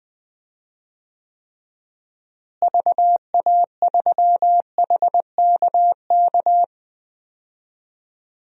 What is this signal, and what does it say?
Morse code, heard as a keyed tone, about 20 wpm, 700 Hz. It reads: VA3HKK
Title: DXCC-100 Call Signs - 6 Characters - Repeated 2x - 20wpm